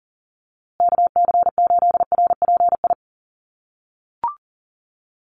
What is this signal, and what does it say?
Morse code, heard as a keyed tone, about 40 wpm, 700 Hz. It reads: KC8RPI
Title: USA Call Signs Repeated 2x 40wpm